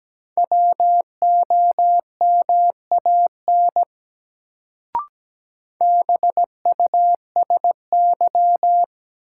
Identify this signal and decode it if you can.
Morse code, heard as a keyed tone, about 17 wpm, 700 Hz. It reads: WOMAN BUSY
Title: Sentences from Top 1000 Words - Review Part 2 - New Word Drill - 17wpm